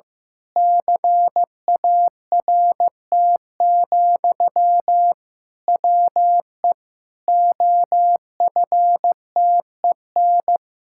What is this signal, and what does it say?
Morse code, heard as a keyed tone, about 15 wpm, 700 Hz. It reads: CART, WE OFTEN
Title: Pangram Challenge v2 - 15wpm